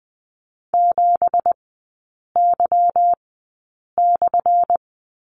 Morse code, heard as a keyed tone, about 20 wpm, spaced as for 8 wpm, 700 Hz. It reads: 7Y/
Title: 212 Lesson 34e - Number 8 - Three Characters 20wpm